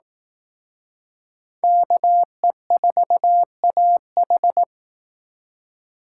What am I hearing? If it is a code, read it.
Morse code, heard as a keyed tone, about 18 wpm, 700 Hz. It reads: KE4AH
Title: Speed Racing - USA 2x2 Call Signs 15wpm